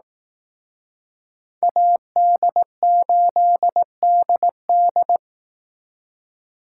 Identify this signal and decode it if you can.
Morse code, heard as a keyed tone, about 18 wpm, 700 Hz. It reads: AD8DD
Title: Speed Racing - USA 2x2 Call Signs 15wpm